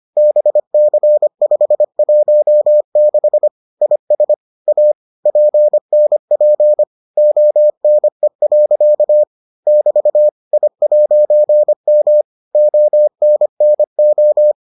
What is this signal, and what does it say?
Morse code, heard as a keyed tone, about 25 wpm, 600 Hz. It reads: BC516 IS A PNP ONE. <BT> I'M ONNO